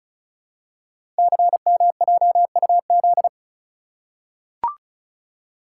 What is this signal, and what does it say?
Morse code, heard as a keyed tone, about 36 wpm, 690 Hz. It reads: CMJUZ
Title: Code Groups - 5 Letters - 35wpm